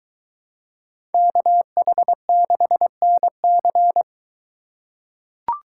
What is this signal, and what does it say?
Morse code, heard as a keyed tone, about 23 wpm, 700 Hz. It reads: KH6NC